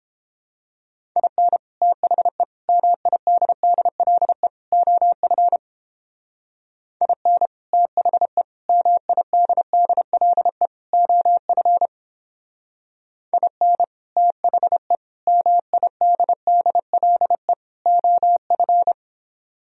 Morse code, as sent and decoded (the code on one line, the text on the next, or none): .. -. / - .... / -- .. -.. -.. .-.. / --- ..-. / .. -. / - .... / -- .. -.. -.. .-.. / --- ..-. / .. -. / - .... / -- .. -.. -.. .-.. / --- ..-.
IN THE MIDDLE OF IN THE MIDDLE OF IN THE MIDDLE OF